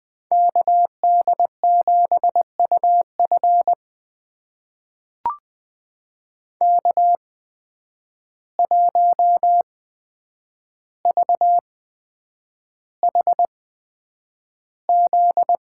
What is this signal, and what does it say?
Morse code, heard as a keyed tone, about 20 wpm, spaced as for 9 wpm, 700 Hz. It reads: KD7UF K1VHZ
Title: USA Call Signs - 5 Characters - 8x Character Spacing - 20wpm